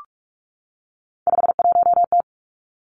Morse code, heard as a keyed tone, about 46 wpm, 700 Hz. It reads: 51T
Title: Three Challenging Letter-Number Combinations 45wpm